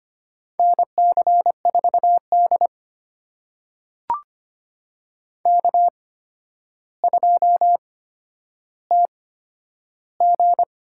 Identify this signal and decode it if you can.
Morse code, heard as a keyed tone, about 25 wpm, spaced as for 11 wpm, 700 Hz. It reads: NC4D K2TG
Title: USA Call Signs - 8x Character Spacing - 25wpm